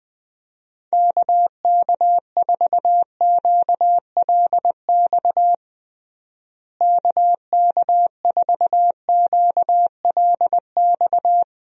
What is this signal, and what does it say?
Morse code, heard as a keyed tone, about 21 wpm, 700 Hz. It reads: KK4QLX KK4QLX
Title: USA Call Signs - 6 Characters - Repeated 3x - 20wpm